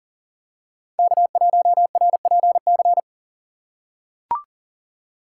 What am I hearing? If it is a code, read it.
Morse code, heard as a keyed tone, about 40 wpm, 700 Hz. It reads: K1RPC
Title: USA Call Signs - 4x Character Spacing - 40wpm